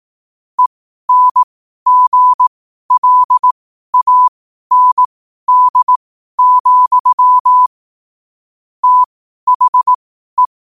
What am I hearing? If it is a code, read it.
Morse code, heard as a keyed tone, about 18 wpm, spaced as for 12 wpm, 1000 Hz. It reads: ENGLAND, THE
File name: qotdfiles / 12wpm